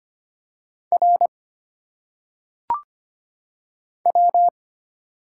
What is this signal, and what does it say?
Morse code, heard as a keyed tone, about 25 wpm, 700 Hz. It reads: R W
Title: Single Letters - Phonetic Pronunciation 25wpm